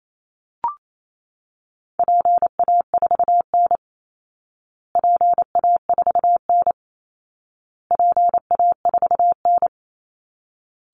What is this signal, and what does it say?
Morse code, heard as a keyed tone, about 28 wpm, 700 Hz. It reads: PA4N PA4N PA4N
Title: Call Signs Repeated 2x - 28wpm